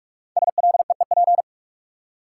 Morse code, heard as a keyed tone, about 42 wpm, 700 Hz. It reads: SLEEP
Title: Five Letter Words - 45wpm